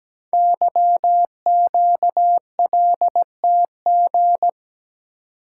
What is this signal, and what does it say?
Morse code, heard as a keyed tone, about 17 wpm, 700 Hz. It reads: YQLTG
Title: Code Groups - 5 Letters - 17wpm